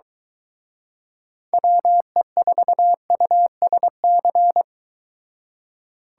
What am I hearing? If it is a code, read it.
Morse code, heard as a keyed tone, about 22 wpm, 700 Hz. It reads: WE4USC